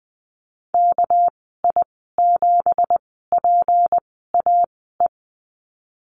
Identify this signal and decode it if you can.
Morse code, heard as a keyed tone, about 20 wpm, spaced as for 15 wpm, 700 Hz. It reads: KI7PAE
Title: 231 Lesson 37g - Letter x - Call Signs 20wpm